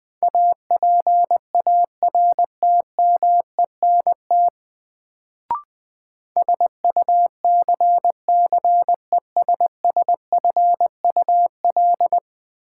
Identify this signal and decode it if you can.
Morse code, heard as a keyed tone, about 20 wpm, 700 Hz. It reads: APARTMENT SUCCESSFUL
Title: Sentences from Top 900 Words - New Word Drill - 20wpm